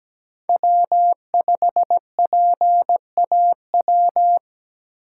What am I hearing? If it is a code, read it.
Morse code, heard as a keyed tone, about 17 wpm, 700 Hz. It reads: W5PAW
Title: USA Call Signs - 5 Characters - 2x Character Spacing - 17wpm